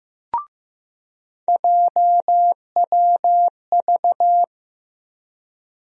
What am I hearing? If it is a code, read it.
Morse code, heard as a keyed tone, about 16 wpm, 700 Hz. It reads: JWV
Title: USA Call Sign Postfixes - 3 Letters 15wpm